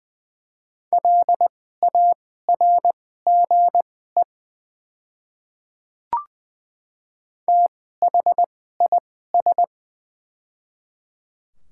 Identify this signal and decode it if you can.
Morse code, heard as a keyed tone, about 20 wpm, spaced as for 14 wpm, 700 Hz. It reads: LARGE THIS
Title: 2x Character Spacing - Top 300 Words - 20wpm